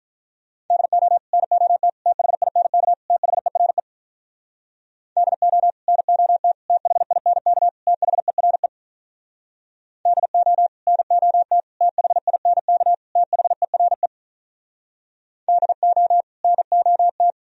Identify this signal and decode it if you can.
Morse code, heard as a keyed tone, about 45 wpm, 700 Hz. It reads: DO NOT THINK THERE DO NOT THINK THERE DO NOT THINK THERE DO NOT